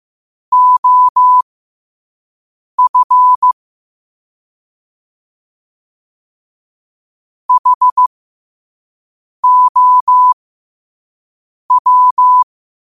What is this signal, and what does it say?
Morse code, heard as a keyed tone, about 15 wpm, spaced as for 5 wpm, 1000 Hz. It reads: OF HOW